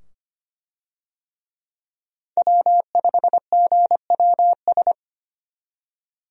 Morse code, heard as a keyed tone, about 25 wpm, 700 Hz. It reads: W5GWS
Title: USA Call Signs - 5 Characters - 25wpm